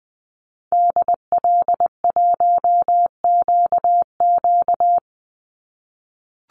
Morse code, heard as a keyed tone, about 20 wpm, 700 Hz. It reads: DL1QQ
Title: Call Signs - 20wpm